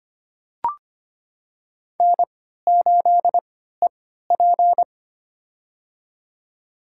Morse code, heard as a keyed tone, about 25 wpm, spaced as for 14 wpm, 700 Hz. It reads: N8EP
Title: USA Call Signs - 4 Characters - 3x Character Spacing - 25wpm